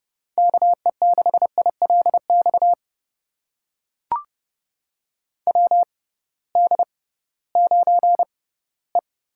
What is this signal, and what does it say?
Morse code, heard as a keyed tone, about 30 wpm, spaced as for 18 wpm, 700 Hz. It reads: KE6ILX WD9E
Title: USA Call Signs - 6 Characters - 6x Character Spacing - 30wpm